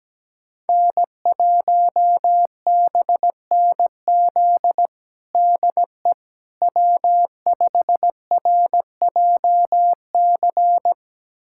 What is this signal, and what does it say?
Morse code, heard as a keyed tone, about 17 wpm, 700 Hz. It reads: N1BNZ DE W5RJC
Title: USA Callsign DE Callsign - 5 Characters - 17wpm